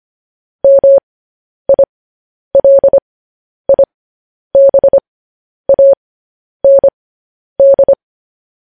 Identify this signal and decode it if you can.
Morse code, heard as a keyed tone, about 25 wpm, spaced as for 10 wpm, 550 Hz. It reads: MILIBAND
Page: news headlines in morse code at 10 wpm